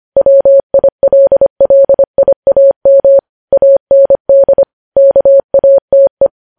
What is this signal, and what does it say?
Morse code, heard as a keyed tone, about 25 wpm, 550 Hz. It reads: WILLIAM AND KATE
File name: podcast / News headlines-1x-25x25